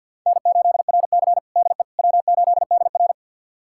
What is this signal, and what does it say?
Morse code, heard as a keyed tone, about 52 wpm, 700 Hz. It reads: N8RC DE W8DR
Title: USA Callsign DE Callsign - 4 Characters - 50wpm